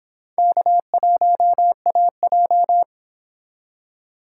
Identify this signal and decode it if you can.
Morse code, heard as a keyed tone, about 31 wpm, 700 Hz. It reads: K1AJ